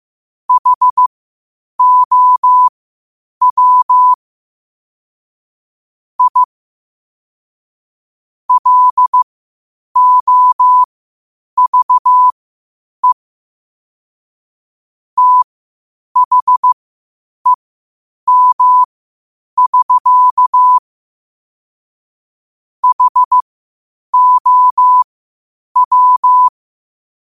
Morse code, heard as a keyed tone, about 15 wpm, spaced as for 8 wpm, 1000 Hz. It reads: HOW I LOVE THEM<SK> HOW
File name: qotdfiles / 8wpm